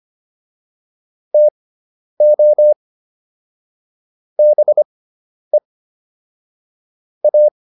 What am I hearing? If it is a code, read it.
Morse code, heard as a keyed tone, about 25 wpm, spaced as for 10 wpm, 600 Hz. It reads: TO BE A